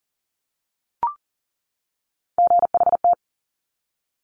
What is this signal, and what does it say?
Morse code, heard as a keyed tone, about 42 wpm, 700 Hz. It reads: GHT
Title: Top Three Letter Combinations in Words 40wpm